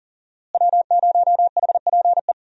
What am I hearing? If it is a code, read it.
Morse code, heard as a keyed tone, about 39 wpm, 700 Hz. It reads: W0HPE
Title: USA Call Signs - 5 Characters - Repeated 3x - 40wpm